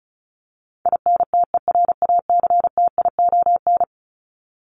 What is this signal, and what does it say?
Morse code, heard as a keyed tone, about 34 wpm, 700 Hz. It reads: INTERACTION
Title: Top 1251-1500 Words 35wpm